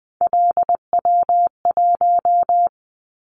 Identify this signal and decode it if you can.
Morse code, heard as a keyed tone, about 20 wpm, 700 Hz. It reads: LW1
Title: Callsign Prefixes for Top 50 DXCC - 20wpm